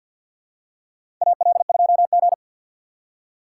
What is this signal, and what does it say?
Morse code, heard as a keyed tone, about 53 wpm, 700 Hz. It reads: AL2G
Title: USA Call Signs - 4 Characters - 3x Character Spacing - 50wpm